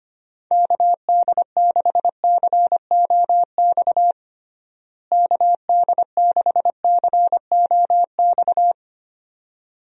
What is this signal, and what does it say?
Morse code, heard as a keyed tone, about 25 wpm, 700 Hz. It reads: KD6COX KD6COX
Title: USA Call Signs - 6 Characters - Repeated 2x - 25wpm